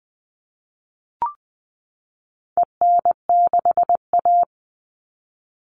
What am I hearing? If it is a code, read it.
Morse code, heard as a keyed tone, about 20 wpm, 700 Hz. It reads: EN6A
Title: All Cut Numbers - 4-Digit 20wpm